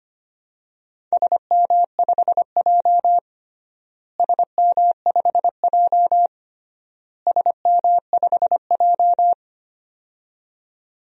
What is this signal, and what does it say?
Morse code, heard as a keyed tone, about 25 wpm, 700 Hz. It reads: SM5J SM5J SM5J
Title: DXCC-100 Call Signs - 4 Characters - Repeated 3x - 25wpm